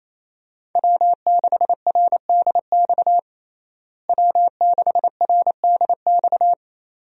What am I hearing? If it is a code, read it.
Morse code, heard as a keyed tone, about 28 wpm, 700 Hz. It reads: W6RDX W6RDX
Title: USA Call Signs - 5 Characters - Repeated 2x - 28wpm